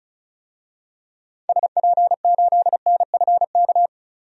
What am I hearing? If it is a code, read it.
Morse code, heard as a keyed tone, about 36 wpm, 700 Hz. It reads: SP8NFK
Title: DXCC-100 Call Signs - 6 Characters - 35wpm